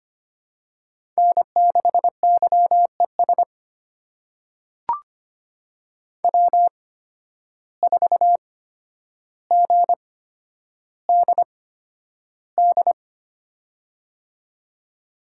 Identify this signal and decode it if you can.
Morse code, heard as a keyed tone, about 25 wpm, spaced as for 11 wpm, 700 Hz. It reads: N6YES W4GDD
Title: USA Call Signs - 5 Characters - 8x Character Spacing - 25wpm